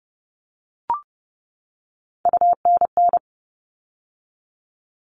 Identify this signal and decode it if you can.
Morse code, heard as a keyed tone, about 30 wpm, 700 Hz. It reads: UNN
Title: All Cut Numbers - 3-Digit 30wpm